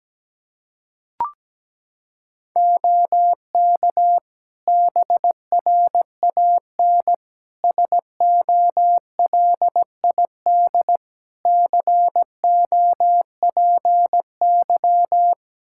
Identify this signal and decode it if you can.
Morse code, heard as a keyed tone, about 17 wpm, 700 Hz. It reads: OK BRAN SOLID COPY
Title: Practice QSO - Exchange 3 - More Info - 17wpm